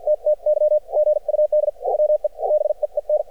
Morse code, heard as a keyed tone, about 52 wpm, spaced as for 44 wpm, 600 Hz. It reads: TTY GUD WE BEEN
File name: archived / CRSnet